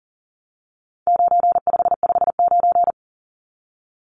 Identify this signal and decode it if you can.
Morse code, heard as a keyed tone, about 37 wpm, 700 Hz. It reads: 9559
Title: Numbers - 4-Digit 40wpm